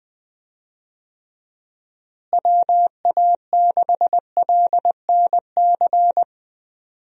Morse code, heard as a keyed tone, about 20 wpm, 700 Hz. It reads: WA6LNC
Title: USA Call Signs - 6 Characters - Repeated 3x - 20wpm